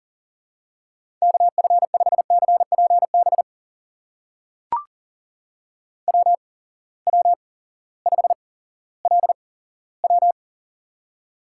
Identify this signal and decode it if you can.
Morse code, heard as a keyed tone, about 42 wpm, spaced as for 19 wpm, 700 Hz. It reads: KF5CPB WW5LW
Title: USA Call Signs - 8x Character Spacing - 40wpm